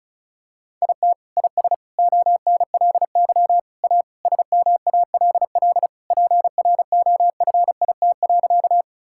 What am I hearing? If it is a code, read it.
Morse code, heard as a keyed tone, about 36 wpm, 700 Hz. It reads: IT IS ONLY A SMALL PROFIT.
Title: Sentences from Top 1000 Words - Review Part 5 - 35wpm